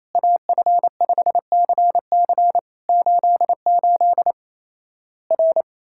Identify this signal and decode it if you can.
Morse code, heard as a keyed tone, about 28 wpm, 700 Hz. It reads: AF5CC 88 R
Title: Worked All Europe DX Contest - QTC Exchange - 28wpm